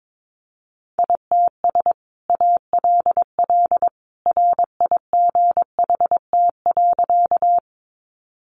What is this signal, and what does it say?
Morse code, heard as a keyed tone, about 22 wpm, 700 Hz. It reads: ITS ALL RIGHT.
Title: Sentences from Top 600 Words 22wpm